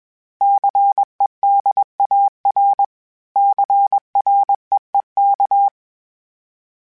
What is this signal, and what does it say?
Morse code, heard as a keyed tone, about 21 wpm, 800 Hz. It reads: CEDAR CREEK
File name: fromSMs / FromSMs mp3s / Cities@20